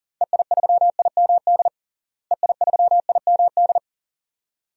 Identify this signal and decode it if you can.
Morse code, heard as a keyed tone, about 40 wpm, 700 Hz. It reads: EI3IMD EI3IMD
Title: DXCC-100 Call Signs - 6 Characters - Repeated 2x - 40wpm